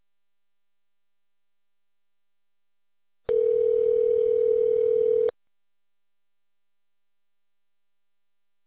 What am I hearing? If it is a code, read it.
Ringback tone (North America), 440 + 480 Hz.